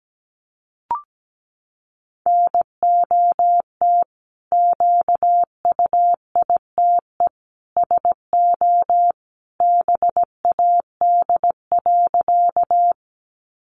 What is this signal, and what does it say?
Morse code, heard as a keyed tone, about 17 wpm, 700 Hz. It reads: NOT QUITE SO BAD.
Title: Sentences from Top 700 Words 17wpm